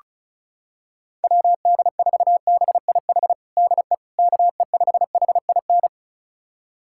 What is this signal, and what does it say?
Morse code, heard as a keyed tone, about 36 wpm, 700 Hz. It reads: WD4BIH DE KE5HIN
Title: USA Callsign DE Callsign 35wpm